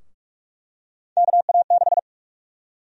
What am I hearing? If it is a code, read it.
Morse code, heard as a keyed tone, about 46 wpm, 700 Hz. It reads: KA6
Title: USA Call Sign Prefixes - 45wpm